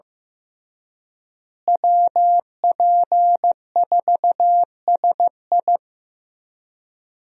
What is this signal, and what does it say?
Morse code, heard as a keyed tone, about 15 wpm, 700 Hz. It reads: WP4SI
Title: USA Call Signs - 5 Characters - Repeated 3x - 15wpm